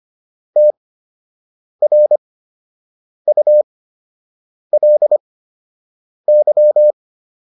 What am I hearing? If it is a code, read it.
Morse code, heard as a keyed tone, about 25 wpm, spaced as for 7 wpm, 600 Hz. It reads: TRULY